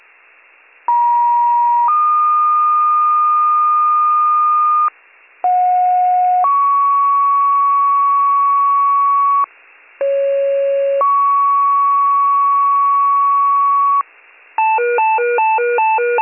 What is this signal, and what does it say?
Two-tone page, heard as a one-second tone, then a three-second tone, about 950 then 1220 Hz, about 730 then 1120 Hz, about 550 then 1120 Hz. It is heard from 0.9 to 14.0 s.